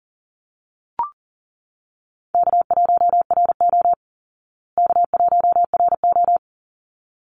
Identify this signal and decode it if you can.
Morse code, heard as a keyed tone, about 40 wpm, 700 Hz. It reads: K1RO K1RO
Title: Easy Call Signs Repeated 2x - 40wpm